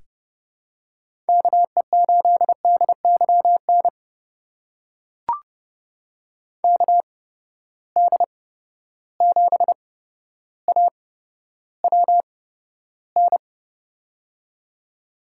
Morse code, heard as a keyed tone, about 30 wpm, spaced as for 13 wpm, 700 Hz. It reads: KE8DYN KD7AWN
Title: USA Call Signs - 6 Characters - 8x Character Spacing - 30wpm